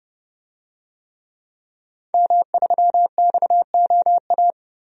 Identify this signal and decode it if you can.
Morse code, heard as a keyed tone, about 30 wpm, 700 Hz. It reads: M3XOA
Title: DXCC-100 Call Signs - 5 Characters - 30wpm